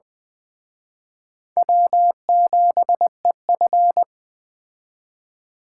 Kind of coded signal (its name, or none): Morse code